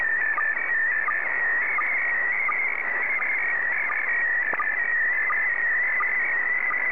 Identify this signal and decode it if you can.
SSTV (slow-scan television), PD 90.
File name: sstv / PD90-ZL2TJM